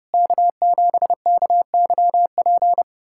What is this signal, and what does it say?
Morse code, heard as a keyed tone, about 30 wpm, 700 Hz. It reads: K7KYP
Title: USA Callsign DE Callsign - 5 Characters - 30wpm